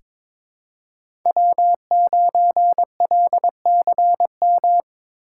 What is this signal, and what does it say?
Morse code, heard as a keyed tone, about 22 wpm, 700 Hz. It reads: W9LCM
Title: USA Call Signs - 5 Characters - 2x Character Spacing - 22wpm